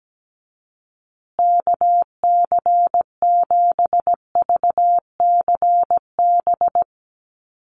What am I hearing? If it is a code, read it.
Morse code, heard as a keyed tone, about 17 wpm, 700 Hz. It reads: KC7VCB
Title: USA Call Signs Repeated 2x 17wpm